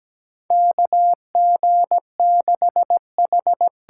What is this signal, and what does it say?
Morse code, heard as a keyed tone, about 17 wpm, 700 Hz. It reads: KG6H
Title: USA Call Signs - 6 Characters - 8x Character Spacing - 17wpm